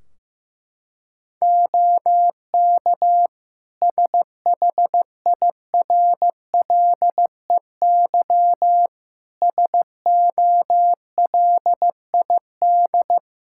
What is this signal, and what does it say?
Morse code, heard as a keyed tone, about 15 wpm, 700 Hz. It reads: OK SHIRLEY SOLID
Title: Practice QSO - Exchange 3 - More Info - 15wpm